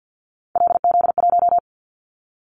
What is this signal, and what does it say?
Morse code, heard as a keyed tone, about 50 wpm, 700 Hz. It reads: LZ1
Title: Callsign Prefixes for Top 50 DXCC - 50wpm